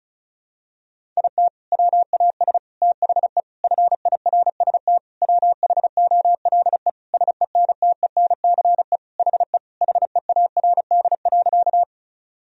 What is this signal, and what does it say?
Morse code, heard as a keyed tone, about 34 wpm, 700 Hz. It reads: IT WAS THE FIRST WHOLE SENTENCE HE HEARD.
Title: Sentences from Top 1000 Words - Review Part 1 - 35wpm